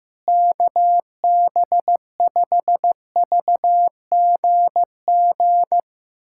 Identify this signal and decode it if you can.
Morse code, heard as a keyed tone, about 15 wpm, 700 Hz. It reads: KB5VGG
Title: USA Callsign DE Callsign - 6 Characters - 15wpm